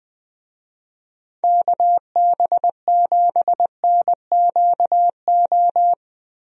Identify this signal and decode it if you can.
Morse code, heard as a keyed tone, about 20 wpm, 700 Hz. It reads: KB7NQO
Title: USA Call Signs - 6 Characters - Repeated 3x - 20wpm